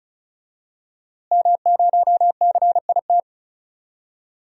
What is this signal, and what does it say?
Morse code, heard as a keyed tone, about 34 wpm, 690 Hz. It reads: M0CIT
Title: DXCC-100 Call Signs - 5 Characters - Repeated 3x - 35wpm